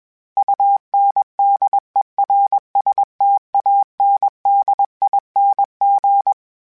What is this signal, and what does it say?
Morse code, heard as a keyed tone, about 21 wpm, 800 Hz. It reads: UNDERSTANDING
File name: Special INGs / Special INGs-mp3s / Sngl Ing @20